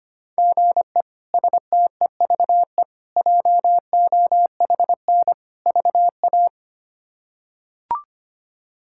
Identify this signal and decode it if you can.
Morse code, heard as a keyed tone, about 25 wpm, 700 Hz. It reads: GE STEVE JOHN VA
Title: K1USN SST QSO Practice - 25wpm